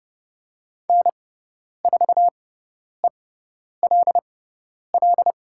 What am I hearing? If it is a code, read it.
Morse code, heard as a keyed tone, about 30 wpm, spaced as for 10 wpm, 690 Hz. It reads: N4ELL
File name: Lesson 17g - Letter u - Call Signs 30wpm